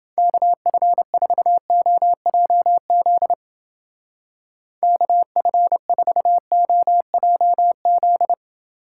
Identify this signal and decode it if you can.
Morse code, heard as a keyed tone, about 31 wpm, 700 Hz. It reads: KF4OJZ KF4OJZ